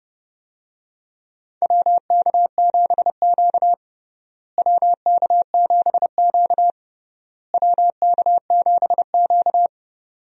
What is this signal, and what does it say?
Morse code, heard as a keyed tone, about 31 wpm, 700 Hz. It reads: WK7Q WK7Q WK7Q
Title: USA Call Signs - 4 Characters - Repeated 3x - 30wpm